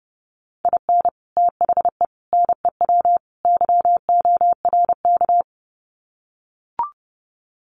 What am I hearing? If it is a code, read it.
Morse code, heard as a keyed tone, about 30 wpm, 700 Hz. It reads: IN THE NEW YORK
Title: Sets of 4 Words 30wpm